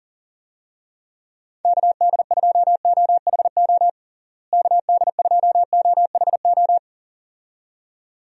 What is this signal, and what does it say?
Morse code, heard as a keyed tone, about 38 wpm, 700 Hz. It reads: KD2OHO KD2OHO
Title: USA Call Signs - 6 Characters - Repeated 2x - 40wpm